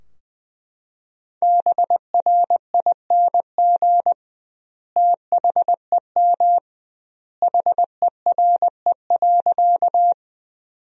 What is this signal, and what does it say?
Morse code, heard as a keyed tone, about 21 wpm, 700 Hz. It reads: BRING THEM HERE.
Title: 2x Word-Spacing - Sentences from Top 500 Words - 20wpm